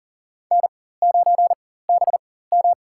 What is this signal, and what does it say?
Morse code, heard as a keyed tone, about 40 wpm, spaced as for 19 wpm, 700 Hz. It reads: N9BM
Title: USA Call Signs - 4x Character Spacing - 40wpm